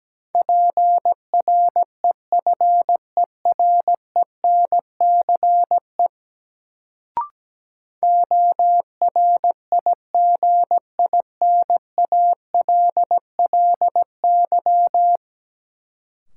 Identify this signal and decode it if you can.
Morse code, heard as a keyed tone, about 17 wpm, 700 Hz. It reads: PREFERENCE ORIGINALLY
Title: Sentences from Top 1500 Words - New Word Drill - 17wpm